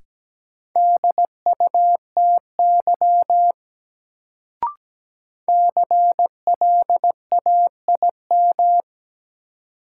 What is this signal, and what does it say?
Morse code, heard as a keyed tone, about 17 wpm, 700 Hz. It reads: DUTY CLAIM
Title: Sentences from Top 1500 Words - New Word Drill Alt - 17wpm